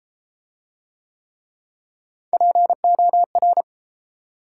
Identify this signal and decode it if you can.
Morse code, heard as a keyed tone, about 31 wpm, 700 Hz. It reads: POR